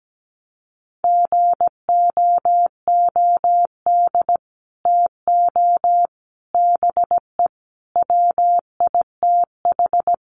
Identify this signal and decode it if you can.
Morse code, heard as a keyed tone, about 17 wpm, 700 Hz. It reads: GOOD TO BE WITH
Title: Sets of 5 Words Encore2 17wpm